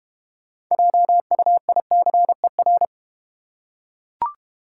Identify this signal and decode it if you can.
Morse code, heard as a keyed tone, about 30 wpm, 690 Hz. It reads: JUICER
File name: Easy Words - Speed-Racing with Hints - 25wpm